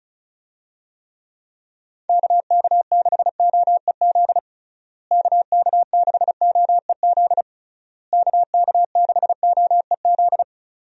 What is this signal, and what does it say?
Morse code, heard as a keyed tone, about 34 wpm, 700 Hz. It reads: KK6OEZ KK6OEZ KK6OEZ
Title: USA Call Signs - 6 Characters - Repeated 3x - 35wpm